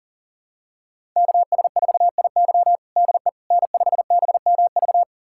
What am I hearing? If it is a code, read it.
Morse code, heard as a keyed tone, about 40 wpm, 700 Hz. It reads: KS4IY DE N5BMV
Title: USA Callsign DE Callsign - 5 Characters - 40wpm